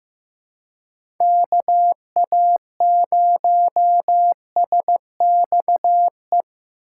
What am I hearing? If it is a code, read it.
Morse code, heard as a keyed tone, about 15 wpm, 700 Hz. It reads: KA0SXE
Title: USA Call Signs - 6 Characters - 3x Character Spacing - 15wpm